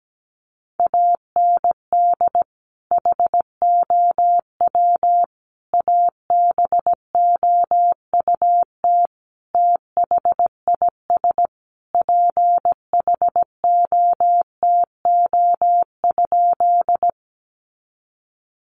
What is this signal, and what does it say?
Morse code, heard as a keyed tone, about 17 wpm, 700 Hz. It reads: AND HOW ABOUT THIS PHOTO?
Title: Sentences from Top 1200 Words 17wpm